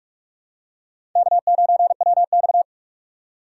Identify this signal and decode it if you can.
Morse code, heard as a keyed tone, about 44 wpm, 700 Hz. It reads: K9WX
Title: USA Call Signs - 2x Character Spacing - 45wpm